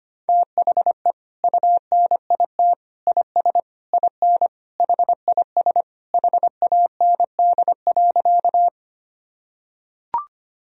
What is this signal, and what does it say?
Morse code, heard as a keyed tone, about 25 wpm, 700 Hz. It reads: THE UNIT IS IN HIS HAND.
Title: Sentences from Top 1000 Words - Review Part 2 - 25wpm